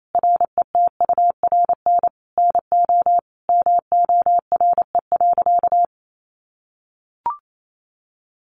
Morse code, heard as a keyed tone, about 28 wpm, 700 Hz. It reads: RETURN NO MORE.
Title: Sentences from Top 800 Words 28wpm